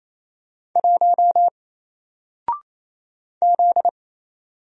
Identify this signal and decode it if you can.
Morse code, heard as a keyed tone, about 28 wpm, 700 Hz. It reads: 1 Z
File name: Single-Letter-Number-28wpm